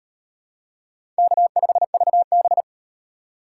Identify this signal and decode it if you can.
Morse code, heard as a keyed tone, about 36 wpm, 700 Hz. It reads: K5VB